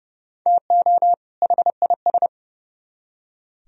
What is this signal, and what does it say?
Morse code, heard as a keyed tone, about 37 wpm, 700 Hz. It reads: TO HIS
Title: Speed-Racing - Sets of 2 Words 30wpm